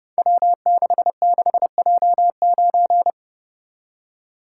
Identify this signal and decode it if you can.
Morse code, heard as a keyed tone, about 28 wpm, 700 Hz. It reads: W66J9
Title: Code Groups - 5 Letter:Numbers - 30wpm